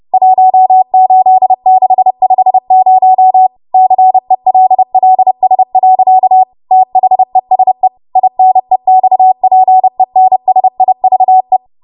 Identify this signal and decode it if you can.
Morse code, heard as a keyed tone, about 30 wpm, 750 Hz. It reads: 18650 CELLS. THESE INEXPENSIVE